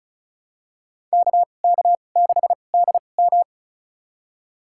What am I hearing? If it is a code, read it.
Morse code, heard as a keyed tone, about 36 wpm, spaced as for 25 wpm, 700 Hz. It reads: KK6DM
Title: USA Call Signs - 5 Characters - 2x Character Spacing - 35wpm